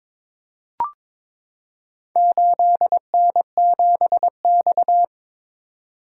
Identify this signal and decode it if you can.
Morse code, heard as a keyed tone, about 22 wpm, 700 Hz. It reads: 8N7X
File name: practice / 22wpm / Four Letter-Number Combinations - 22wpm